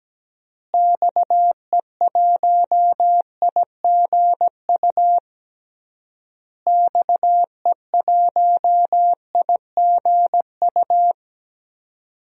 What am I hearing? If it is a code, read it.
Morse code, heard as a keyed tone, about 17 wpm, 700 Hz. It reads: XE1IGU XE1IGU
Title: DXCC-100 Call Signs - 6 Characters - Repeated 3x - 17wpm